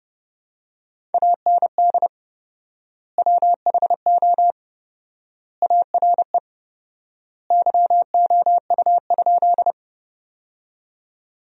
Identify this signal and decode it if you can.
Morse code, heard as a keyed tone, about 30 wpm, 700 Hz. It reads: AND WHO ARE YOU?
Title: 4x Word-Spacing - Sentences from Top 100 Words - 30wpm